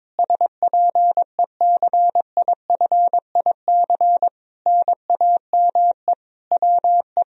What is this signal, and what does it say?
Morse code, heard as a keyed tone, about 22 wpm, 700 Hz. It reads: SPECIFIC NAME WE
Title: Sentences from Top 1000 Words - Review Part 3 - 22wpm